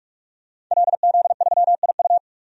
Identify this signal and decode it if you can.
Morse code, heard as a keyed tone, about 42 wpm, 700 Hz. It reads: LZ3IU
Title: DXCC-100 Call Signs - 5 Characters - 45wpm